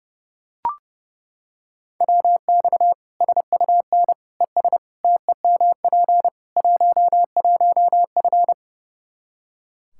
Morse code, heard as a keyed tone, about 30 wpm, 700 Hz. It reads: WX SUN ES TEMP 11F
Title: Practice QSO - Exchange 2 - Info - 30wpm